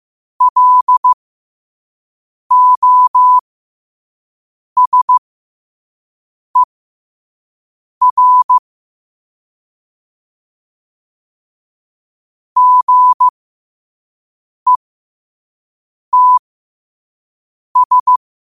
Morse code, heard as a keyed tone, about 15 wpm, spaced as for 5 wpm, 1000 Hz. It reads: LOSER GETS
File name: qotdfiles / 5wpm